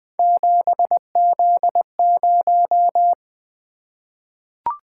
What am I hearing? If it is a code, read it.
Morse code, heard as a keyed tone, about 20 wpm, 700 Hz. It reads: 7Z0
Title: Three Letter-Number Combinations - 20wpm